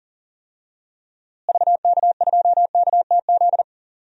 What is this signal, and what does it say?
Morse code, heard as a keyed tone, about 40 wpm, 700 Hz. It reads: VK2KTZ
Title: DXCC-100 Call Signs - 6 Characters - Repeated 2x - 40wpm